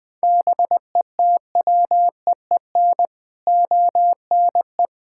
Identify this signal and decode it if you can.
Morse code, heard as a keyed tone, about 20 wpm, 700 Hz. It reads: BETWEEN ONE
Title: Sets of 6 Words - 20wpm